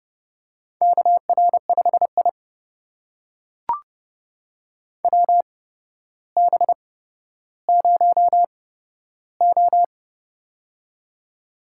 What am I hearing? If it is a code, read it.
Morse code, heard as a keyed tone, about 30 wpm, spaced as for 13 wpm, 700 Hz. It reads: KR5I WB0O
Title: USA Call Signs - 4 Characters - 8x Character Spacing - 30wpm